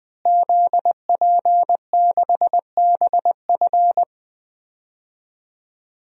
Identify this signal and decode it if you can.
Morse code, heard as a keyed tone, about 20 wpm, 700 Hz. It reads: ZP6BF
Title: DXCC-100 Call Signs - 5 Characters - Repeated 3x - 20wpm